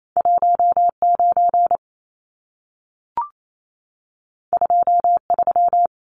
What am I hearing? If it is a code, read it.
Morse code, heard as a keyed tone, about 28 wpm, 700 Hz. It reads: 19 23
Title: Numbers - 2-Digit 28wpm